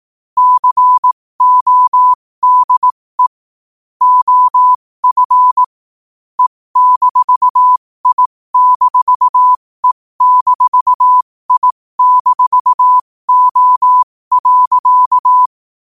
Morse code, heard as a keyed tone, about 18 wpm, spaced as for 15 wpm, 1000 Hz. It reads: CODE OF E-I-E-I-O.
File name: qotdfiles / 15wpm